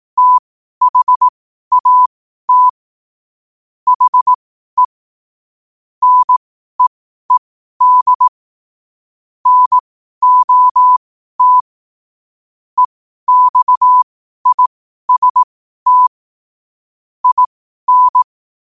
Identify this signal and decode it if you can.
Morse code, heard as a keyed tone, about 18 wpm, spaced as for 12 wpm, 1000 Hz. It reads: THAT HE NEED NOT EXIST IN
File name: 2024-09-14-12wpm